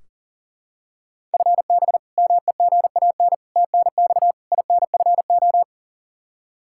Morse code, heard as a keyed tone, about 40 wpm, 700 Hz. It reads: FB MEGAN TNX INFO
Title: Practice QSO - Exchange 3 - More Info - 40wpm